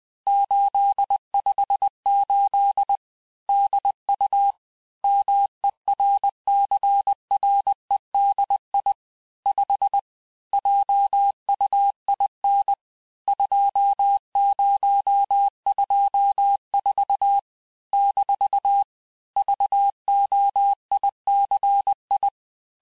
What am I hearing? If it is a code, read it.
Morse code, heard as a keyed tone, about 20 wpm, 800 Hz. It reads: 858 DU MERCREDI 5 JUIN 2024 - VOICI
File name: cours-858-texte-chiffre-vitesse-19mpm